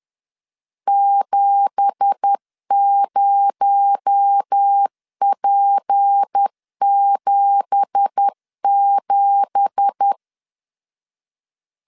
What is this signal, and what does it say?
Morse code, heard as a keyed tone, about 10 wpm, 800 Hz. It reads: 70P77
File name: LEZIONI / 9 2867 10wpm 2733